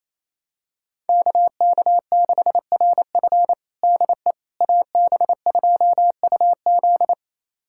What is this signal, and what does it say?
Morse code, heard as a keyed tone, about 28 wpm, 700 Hz. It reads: KK6RF DE AB2UZ
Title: USA Callsign DE Callsign - 5 Characters - 28wpm